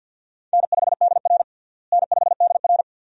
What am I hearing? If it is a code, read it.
Morse code, heard as a keyed tone, about 52 wpm, 700 Hz. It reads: N5DR N5DR
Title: USA Call Signs - 4 Characters - Repeated 2x - 50wpm